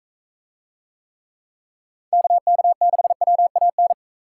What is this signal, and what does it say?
Morse code, heard as a keyed tone, about 42 wpm, 700 Hz. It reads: KK6WAN